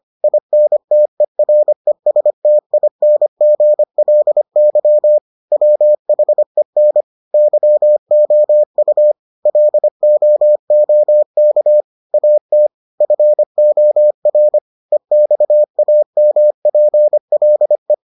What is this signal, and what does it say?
Morse code, heard as a keyed tone, about 25 wpm, 600 Hz. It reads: INTERESTINGLY WHEN YOU LOOK AT FOR EXAMPLE